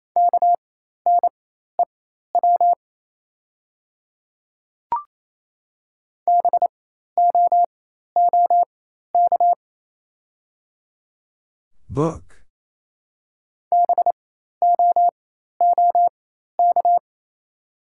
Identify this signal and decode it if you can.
Morse code, heard as a keyed tone, about 27 wpm, spaced as for 13 wpm, 700 Hz. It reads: KNEW BOOK BOOK
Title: 4x Character Spacing - Top 500 Words - 28wpm